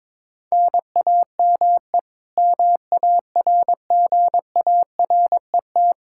Morse code, heard as a keyed tone, about 22 wpm, 700 Hz. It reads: NAME MARGARET
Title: K1USN SST QSO Practice 22wpm